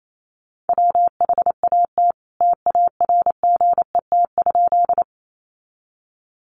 Morse code, heard as a keyed tone, about 28 wpm, 700 Hz. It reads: WHAT TARGET?
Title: Sentences from Top 1200 Words 28wpm